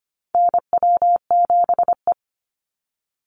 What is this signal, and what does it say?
Morse code, heard as a keyed tone, about 25 wpm, 700 Hz. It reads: NW7E
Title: NA SOTA Chaser Callsigns 25wpm